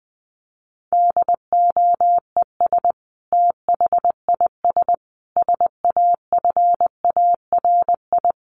Morse code, heard as a keyed tone, about 20 wpm, 700 Hz. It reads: DOES THIS SAFARI
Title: A Sound Of Thunder 20wpm